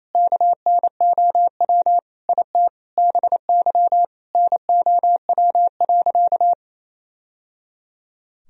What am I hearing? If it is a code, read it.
Morse code, heard as a keyed tone, about 28 wpm, 700 Hz. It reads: KNOW IT BY NOW.
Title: Sentences from Top 1000 Words - Review Part 2 - 28wpm